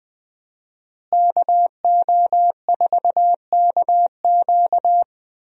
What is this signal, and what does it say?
Morse code, heard as a keyed tone, about 20 wpm, 700 Hz. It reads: KO4KQ
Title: USA Call Signs - 5 Characters - Repeated 3x - 20wpm